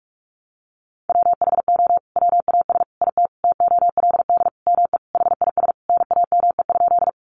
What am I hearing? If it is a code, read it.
Morse code, heard as a keyed tone, about 44 wpm, 700 Hz. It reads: WHO WAS IT TOLD ME HIS NAME?
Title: Sentences from Top 400 Words 45wpm